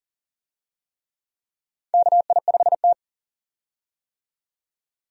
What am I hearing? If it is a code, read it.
Morse code, heard as a keyed tone, about 40 wpm, 700 Hz. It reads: KI5T
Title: USA Call Signs - 4 Characters - 40wpm